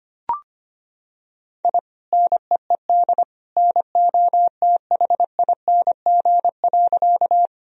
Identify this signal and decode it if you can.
Morse code, heard as a keyed tone, about 25 wpm, 700 Hz. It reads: I NEED NOTHING.
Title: Sentences from Top 1000 Words - Review Part 1 - 25wpm